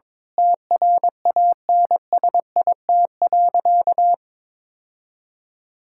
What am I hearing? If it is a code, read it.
Morse code, heard as a keyed tone, about 22 wpm, 700 Hz. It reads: TRANSIT.
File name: Difficult Sentences with Hints - 22wpm